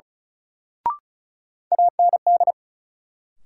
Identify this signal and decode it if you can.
Morse code, heard as a keyed tone, about 36 wpm, 700 Hz. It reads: AND
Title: Sets of 2 Words - New Word Drill - 35wpm